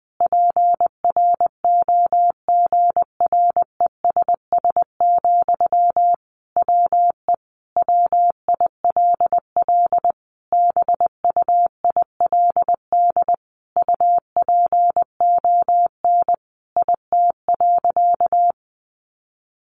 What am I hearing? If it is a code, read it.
Morse code, heard as a keyed tone, about 20 wpm, 700 Hz. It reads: PROGRESS, WE WILL BUILD UPON IT.
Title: Sentences from Top 1400 Words 20wpm